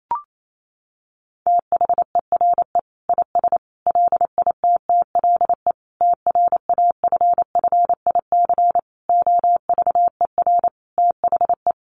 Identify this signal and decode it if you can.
Morse code, heard as a keyed tone, about 28 wpm, 700 Hz. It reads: THERE IS LITTLE TRAFFIC OVER THE
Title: Sentences from Top 1300 Words 28wpm